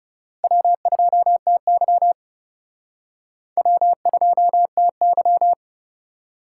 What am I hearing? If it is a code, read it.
Morse code, heard as a keyed tone, about 35 wpm, 700 Hz. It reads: W2TY W2TY
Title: Speed Racing - USA 1x2 Call Signs 30wpm